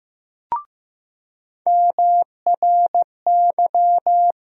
Morse code, heard as a keyed tone, about 15 wpm, 700 Hz. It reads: MRY
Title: USA Call Sign Postfixes - 3 Letters 15wpm